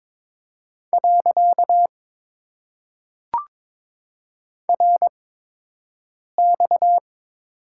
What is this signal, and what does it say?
Morse code, heard as a keyed tone, about 22 wpm, 690 Hz. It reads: . R X
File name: Top 90 QSO Elements - 8x Character Spacing - 22wpm